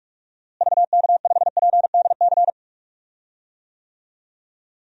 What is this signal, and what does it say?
Morse code, heard as a keyed tone, about 46 wpm, 700 Hz. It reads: VK5PDC
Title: DXCC-100 Call Signs - 6 Characters - Repeated 2x - 45wpm